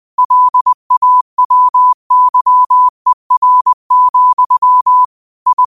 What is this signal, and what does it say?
Morse code, heard as a keyed tone, about 20 wpm, 1000 Hz. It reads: LAWYER, I
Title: 20 WPM morse code quotes for Thu, 29 Feb 2024